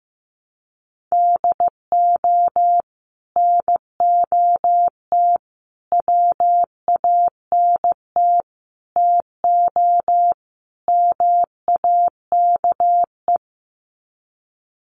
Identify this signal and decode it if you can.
Morse code, heard as a keyed tone, about 15 wpm, 700 Hz. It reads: DO NOT WANT TO MAKE
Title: Sets of 5 Words Encore 15wpm